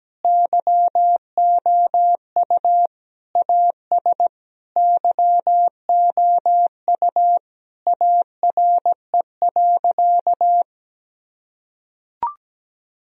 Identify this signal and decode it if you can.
Morse code, heard as a keyed tone, about 17 wpm, 700 Hz. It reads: YOU AS YOU ARE.
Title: Sentences from Top 1000 Words - Review Part 4 - 17wpm